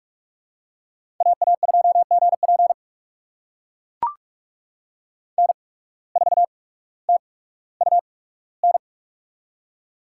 Morse code, heard as a keyed tone, about 44 wpm, spaced as for 19 wpm, 700 Hz. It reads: AA2GP N4TUN
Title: USA Call Signs - 5 Characters - 8x Character Spacing - 45wpm